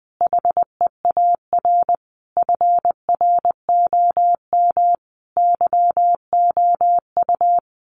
Morse code, heard as a keyed tone, about 19 wpm, 700 Hz. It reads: HEAR FROM YOU
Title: Common QSO Phrases 20wpm